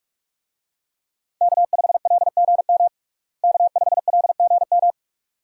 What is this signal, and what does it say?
Morse code, heard as a keyed tone, about 42 wpm, 700 Hz. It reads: K5LGM K5LGM
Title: USA Call Signs - 5 Characters - Repeated 2x - 45wpm